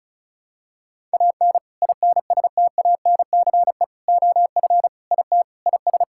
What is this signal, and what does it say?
Morse code, heard as a keyed tone, about 36 wpm, 700 Hz. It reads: AN INSTANCE OF IT IS
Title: Sentences from Top 1000 Words - Review Part 5 - 35wpm